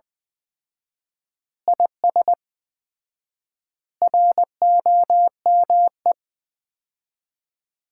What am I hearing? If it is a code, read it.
Morse code, heard as a keyed tone, about 20 wpm, 700 Hz. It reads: IS ROME
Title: Common QSO Phrases - 4x Word-Spacing - 20wpm